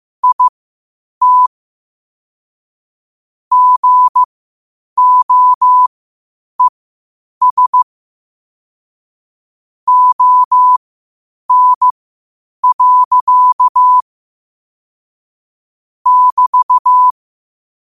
Morse code, heard as a keyed tone, about 15 wpm, spaced as for 8 wpm, 1000 Hz. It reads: IT GOES ON. <BT>
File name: qotdfiles / 8wpm